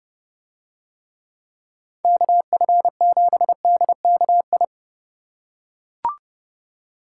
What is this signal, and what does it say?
Morse code, heard as a keyed tone, about 29 wpm, 700 Hz. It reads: KF7DKI